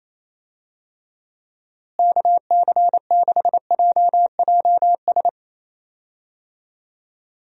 Morse code, heard as a keyed tone, about 28 wpm, 700 Hz. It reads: KC6JJS
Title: USA Call Signs - 6 Characters - Repeated 2x - 28wpm